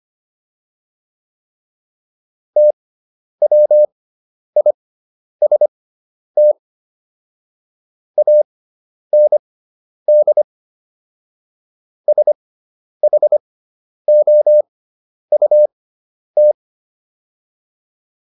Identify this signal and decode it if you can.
Morse code, heard as a keyed tone, about 25 wpm, spaced as for 10 wpm, 600 Hz. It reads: TWIST AND SHOUT